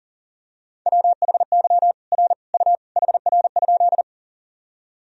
Morse code, heard as a keyed tone, about 40 wpm, 700 Hz. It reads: WHY R U HR?
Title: Sentences from Top 200 Words and Abbreviations 40wpm